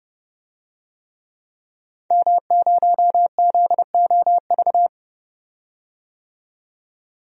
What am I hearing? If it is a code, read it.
Morse code, heard as a keyed tone, about 30 wpm, 700 Hz. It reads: M0ZOV
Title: DXCC-100 Call Signs - 5 Characters - 30wpm